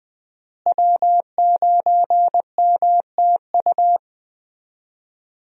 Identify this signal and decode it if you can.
Morse code, heard as a keyed tone, about 20 wpm, 700 Hz. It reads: W9MTU
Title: USA Call Signs - 5 Characters - Repeated 3x - 20wpm